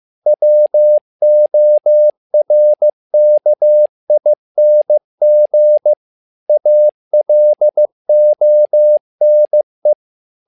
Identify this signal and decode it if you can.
Morse code, heard as a keyed tone, about 15 wpm, 600 Hz. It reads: WORKING ALONE
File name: learning center / tower trouble / tower trouble 15wpm